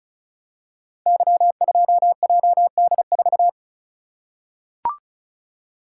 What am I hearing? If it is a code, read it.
Morse code, heard as a keyed tone, about 36 wpm, 700 Hz. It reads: Y2JD4
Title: Code Groups - 5 Letter:Numbers - 35wpm